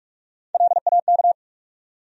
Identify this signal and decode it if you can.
Morse code, heard as a keyed tone, about 42 wpm, 700 Hz. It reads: LAK
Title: USA Call Sign Postfixes - 3 Letters 45wpm